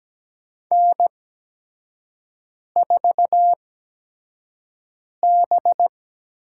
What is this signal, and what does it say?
Morse code, heard as a keyed tone, about 17 wpm, spaced as for 5 wpm, 700 Hz. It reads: N4B
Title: USA Call Signs - 5 Characters - 8x Character Spacing - 17wpm